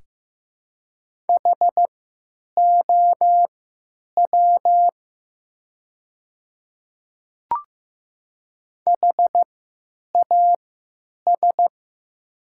Morse code, heard as a keyed tone, about 15 wpm, spaced as for 9 wpm, 700 Hz. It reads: HOW HAS
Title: 3x Character Spacing - Top 100 Words - 15wpm